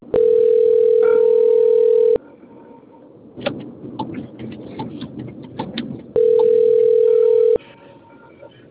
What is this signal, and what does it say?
Ringback tone (North America), 440 + 480 Hz.
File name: out- - - - - .97208